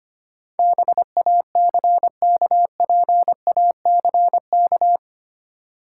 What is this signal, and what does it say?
Morse code, heard as a keyed tone, about 25 wpm, 700 Hz. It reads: BACKPACK
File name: practice / 25wpm / Easy Words with Hints - 25wpm